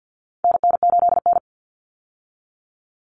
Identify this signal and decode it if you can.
Morse code, heard as a keyed tone, about 50 wpm, 700 Hz. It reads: NN8N
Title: Common Cut Numbers - 4-Digit 50wpm